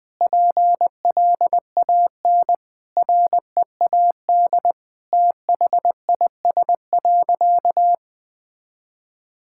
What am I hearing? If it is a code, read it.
Morse code, heard as a keyed tone, about 20 wpm, 700 Hz. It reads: PLAN READ THIS.
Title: Sentences with 8-Word Phrases - Spoken Beginning - 20wpm